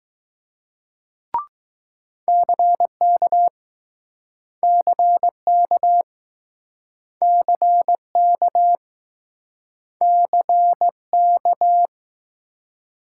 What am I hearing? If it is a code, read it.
Morse code, heard as a keyed tone, about 21 wpm, 700 Hz. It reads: CK CK CK CK